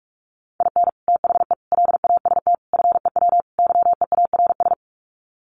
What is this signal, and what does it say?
Morse code, heard as a keyed tone, about 44 wpm, 700 Hz. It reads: IN THE LAST FEW YEARS
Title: Sets of 5 Words 45wpm